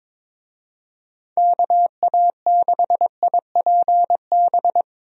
Morse code, heard as a keyed tone, about 22 wpm, 700 Hz. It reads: KA6IPB
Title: USA Call Signs - 6 Characters - Repeated 2x - 22wpm